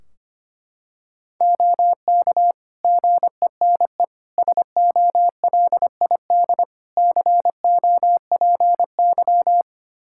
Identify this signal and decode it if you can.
Morse code, heard as a keyed tone, about 25 wpm, 700 Hz. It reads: OK GENE SOLID COPY
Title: Practice QSO - Exchange 3 - More Info - 25wpm